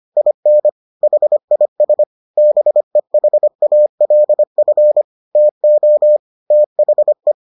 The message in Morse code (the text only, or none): IN HIS BEHALF TO THE